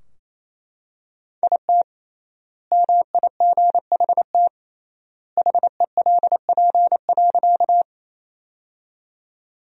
Morse code, heard as a keyed tone, about 28 wpm, 700 Hz. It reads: IT MIGHT HELP.
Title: 3x Word-Spacing - Sentences from Top 300 Words - 28wpm